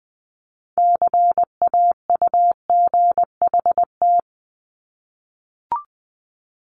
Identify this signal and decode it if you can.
Morse code, heard as a keyed tone, about 20 wpm, 700 Hz. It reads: CAUGHT
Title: Multi-letter Phonemes 20wpm